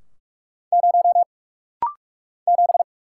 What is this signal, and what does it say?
Morse code, heard as a keyed tone, about 44 wpm, 700 Hz. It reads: 0 7
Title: Single Letter-Number - Rapid-Fire 45wpm